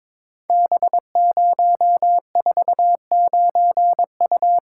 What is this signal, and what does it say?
Morse code, heard as a keyed tone, about 22 wpm, 700 Hz. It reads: B049U
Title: Code Groups - 5 Letter:Numbers - 22wpm